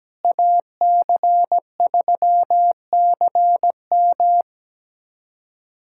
Morse code, heard as a keyed tone, about 17 wpm, 700 Hz. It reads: AC3CM
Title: USA Callsign DE Callsign - 5 Characters - 17wpm